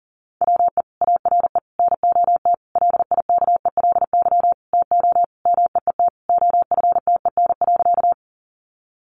Morse code, heard as a keyed tone, about 40 wpm, 700 Hz. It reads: WE ARE NOT LIKELY TO MEET OFTEN.
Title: Sentences from Top 1100 Words 40wpm